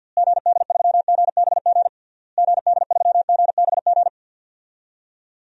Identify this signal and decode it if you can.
Morse code, heard as a keyed tone, about 49 wpm, 700 Hz. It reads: GD3GBG GD3GBG
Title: DXCC-100 Call Signs - 6 Characters - Repeated 3x - 50wpm